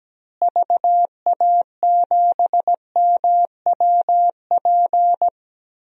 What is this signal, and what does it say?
Morse code, heard as a keyed tone, about 17 wpm, 700 Hz. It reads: VA7MWP
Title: DXCC-100 Call Signs - 6 Characters - Repeated 2x - 17wpm